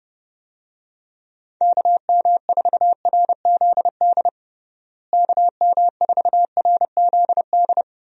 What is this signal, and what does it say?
Morse code, heard as a keyed tone, about 30 wpm, 700 Hz. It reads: KM4RZD KM4RZD
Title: USA Call Signs - 6 Characters - Repeated 2x - 30wpm